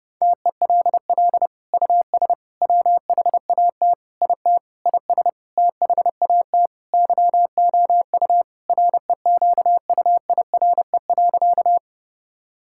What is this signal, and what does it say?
Morse code, heard as a keyed tone, about 30 wpm, 700 Hz. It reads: TELL US WHAT IT IS THAT YOU REQUIRE.
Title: Sentences from Top 1000 Words - Review Part 6 - 30wpm